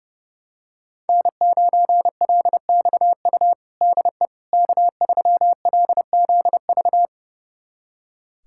Morse code, heard as a keyed tone, about 30 wpm, 700 Hz. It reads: N9LXU DE K3LZV
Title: USA Callsign DE Callsign - 5 Characters - 30wpm